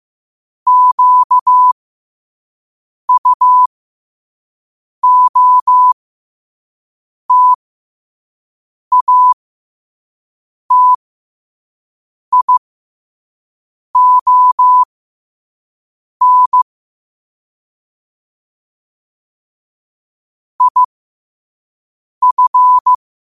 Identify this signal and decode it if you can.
Morse code, heard as a keyed tone, about 15 wpm, spaced as for 5 wpm, 1000 Hz. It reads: QUOTATION IF
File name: 5wpm